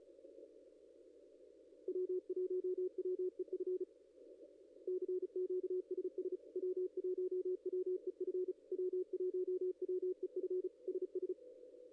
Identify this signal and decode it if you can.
Morse code, heard as a keyed tone, about 34 wpm, 370 Hz, with radio noise. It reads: W1WEF CQ SS W1WEF W1WEF SS
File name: contests / ss-cw-k5tr-op-n5rz -0600